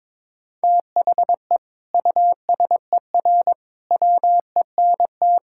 Morse code, heard as a keyed tone, about 22 wpm, 700 Hz. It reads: THE USER WENT
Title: Sentences from Top 1000 Words - Review Part 5 - 22wpm